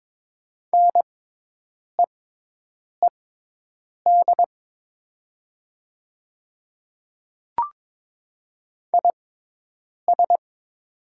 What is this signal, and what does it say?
Morse code, heard as a keyed tone, about 21 wpm, spaced as for 8 wpm, 700 Hz. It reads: NEED IS
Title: 6x Character Spacing - Top 300 Words - 22wpm